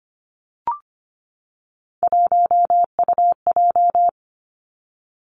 Morse code, heard as a keyed tone, about 25 wpm, 700 Hz. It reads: 1UJ
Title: Three Challenging Letter-Number Combinations 25wpm